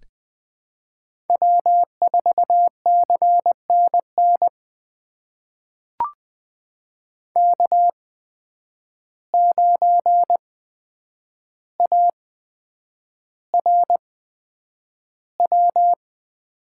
Morse code, heard as a keyed tone, about 20 wpm, spaced as for 9 wpm, 700 Hz. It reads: W4CNN K9ARW
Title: USA Call Signs - 5 Characters - 8x Character Spacing - 20wpm